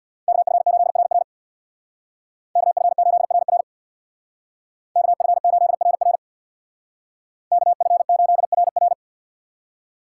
Morse code, heard as a keyed tone, about 64 wpm, 700 Hz. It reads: KF8RR KF8RR KF8RR KF8RR